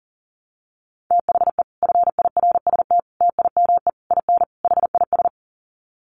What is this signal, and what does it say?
Morse code, heard as a keyed tone, about 40 wpm, 700 Hz. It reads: THE FIRST TIME IN HIS
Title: Sets of 5 Words Encore2 40wpm